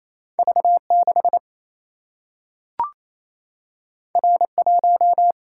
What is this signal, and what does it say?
Morse code, heard as a keyed tone, about 28 wpm, 700 Hz. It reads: V 6 R 1
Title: Single-Letter Single-Number 28wpm